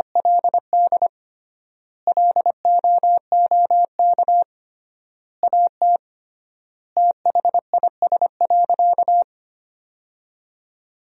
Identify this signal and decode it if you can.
Morse code, heard as a keyed tone, about 25 wpm, 700 Hz. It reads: LD LOOK AT THIS.
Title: 3x Word-Spacing - Sentences from Top 300 Words - 25wpm